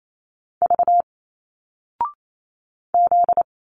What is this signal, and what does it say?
Morse code, heard as a keyed tone, about 28 wpm, 700 Hz. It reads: V Z
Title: Single Letters 28wpm